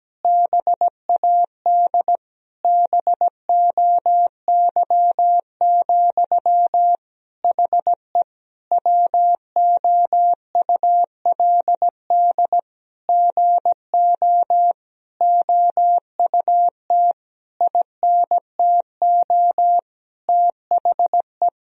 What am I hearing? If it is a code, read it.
Morse code, heard as a keyed tone, about 17 wpm, 700 Hz. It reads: BAD BOY, HE WOULD GO OUT INTO THE
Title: Sentences from Top 1000 Words - Review Part 3 - 17wpm